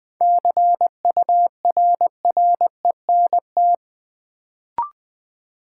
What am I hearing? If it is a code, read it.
Morse code, heard as a keyed tone, about 21 wpm, 700 Hz. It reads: CURRENT
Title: Sentences from Top 1000 Words - Review Part 4 - New Word Drill - 20wpm